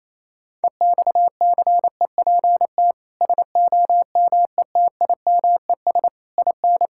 Morse code, heard as a keyed tone, about 28 wpm, 700 Hz. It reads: EXCEPT SOMETIMES IN
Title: Sentences from Top 1600 Words 28wpm